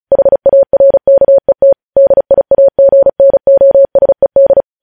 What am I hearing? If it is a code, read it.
Morse code, heard as a keyed tone, about 36 wpm, 550 Hz. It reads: HARKET DIAGNOSED
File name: podcast / News headlines-1x-35x35